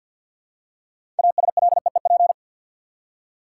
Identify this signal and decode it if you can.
Morse code, heard as a keyed tone, about 49 wpm, 700 Hz. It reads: ASLEEP
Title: Six Letter Words - 50wpm